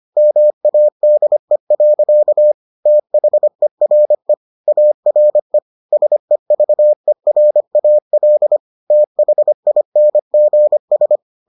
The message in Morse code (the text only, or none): MADE. THERE ARE SEVERAL THINGS